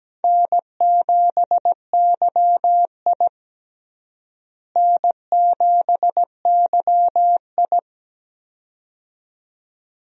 Morse code, heard as a keyed tone, about 17 wpm, 700 Hz. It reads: N7YI N7YI
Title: USA Call Signs - 4 Characters - Repeated 2x - 17wpm